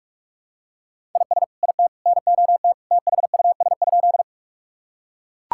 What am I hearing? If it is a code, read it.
Morse code, heard as a keyed tone, about 41 wpm, 700 Hz. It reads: IS IT NOT THUS?
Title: Sentences from Top 1000 Words - Review Part 5 - 45wpm